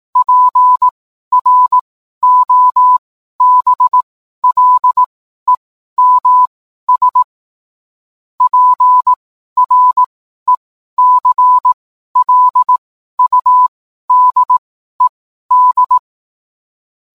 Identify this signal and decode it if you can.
Morse code, heard as a keyed tone, about 18 wpm, spaced as for 12 wpm, 1000 Hz. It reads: PROBLEMS PRECLUDED
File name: cwblog / cw mp3 directory / mem morse 12